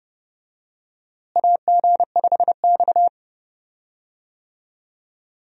Morse code, heard as a keyed tone, about 30 wpm, 700 Hz. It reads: AG5X
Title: USA Call Signs - 4 Characters - Repeated 3x - 30wpm